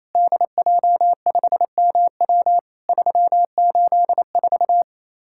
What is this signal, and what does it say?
Morse code, heard as a keyed tone, about 28 wpm, 700 Hz. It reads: DJ5MW 384
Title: Worked All Europe DX Contest - QTC Exchange - 28wpm